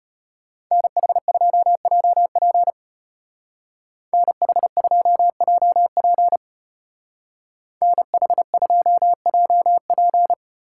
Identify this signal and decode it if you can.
Morse code, heard as a keyed tone, about 32 wpm, 700 Hz. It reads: NH2JP NH2JP NH2JP